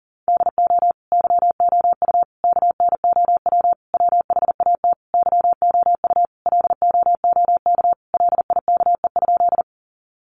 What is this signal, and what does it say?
Morse code, heard as a keyed tone, about 40 wpm, 700 Hz. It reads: DO YOU KNOW WHAT YOU LOOK LIKE?
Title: Sentences from Top 100 Words 40wpm